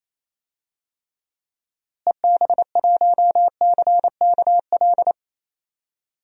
Morse code, heard as a keyed tone, about 28 wpm, 700 Hz. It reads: EB1CKL
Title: DXCC-100 Call Signs - 6 Characters - Repeated 2x - 28wpm